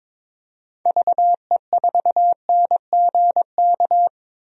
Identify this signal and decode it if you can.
Morse code, heard as a keyed tone, about 22 wpm, 700 Hz. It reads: VE4NGK
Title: DXCC-100 Call Signs - 6 Characters - 22wpm